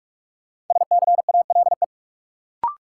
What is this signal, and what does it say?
Morse code, heard as a keyed tone, about 46 wpm, 700 Hz. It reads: SCALE
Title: Sentences from Top 1000 Words - Review Part 4 - New Word Drill - 45wpm